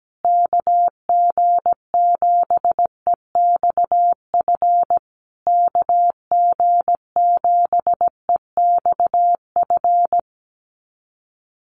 Morse code, heard as a keyed tone, about 17 wpm, 700 Hz. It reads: KG7EXF KG7EXF
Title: USA Call Signs Repeated 3x 17wpm